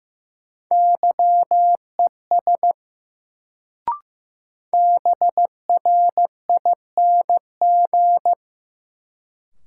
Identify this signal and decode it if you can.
Morse code, heard as a keyed tone, about 15 wpm, 700 Hz. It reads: YES BRING
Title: Sentences from Top 500 Words - New Word Drill - 15wpm